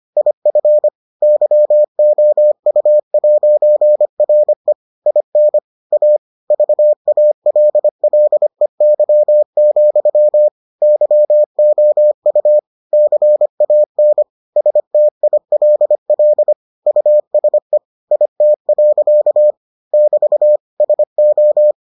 Morse code, heard as a keyed tone, about 25 wpm, 600 Hz. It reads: IF YOU'RE IN A VALLEY, YOU CAN STILL USE IT. <BT> SO